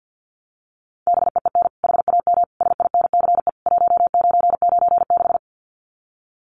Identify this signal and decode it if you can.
Morse code, heard as a keyed tone, about 50 wpm, 700 Hz. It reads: BEEN HAM SINCE 1996
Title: Common QSO Phrases 50wpm